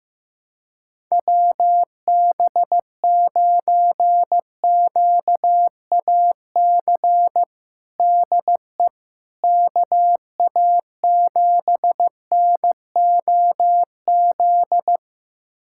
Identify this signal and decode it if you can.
Morse code, heard as a keyed tone, about 15 wpm, 700 Hz. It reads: WB9QAC DE KA7NOZ
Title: USA Callsign DE Callsign - 6 Characters - 15wpm